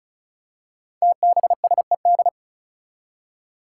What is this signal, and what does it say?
Morse code, heard as a keyed tone, about 36 wpm, 700 Hz. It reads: TBSED
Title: Five Challenging Letter-Number Combinations 35wpm